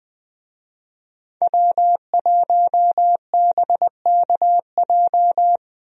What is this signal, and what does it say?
Morse code, heard as a keyed tone, about 21 wpm, 700 Hz. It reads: W1BKJ
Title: USA Call Signs - 5 Characters - Repeated 2x - 20wpm